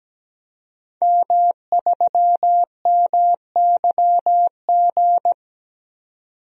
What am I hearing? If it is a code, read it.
Morse code, heard as a keyed tone, about 17 wpm, 700 Hz. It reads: M3MYG
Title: DXCC-100 Call Signs - 5 Characters - 17wpm